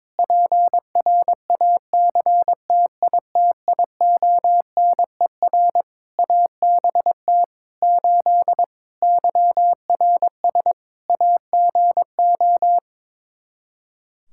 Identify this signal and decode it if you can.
Morse code, heard as a keyed tone, about 22 wpm, 700 Hz. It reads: PRACTITIONER ABT 8 YRS AGO
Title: Practice QSO - Exchange 3 - More Info - 22wpm